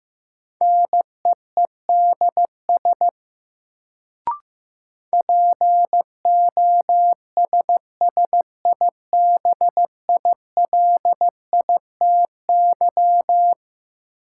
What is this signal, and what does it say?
Morse code, heard as a keyed tone, about 16 wpm, 700 Hz. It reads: NEEDS POSSIBILITY
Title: Sets of 3 Words - New Word Drill - 15wpm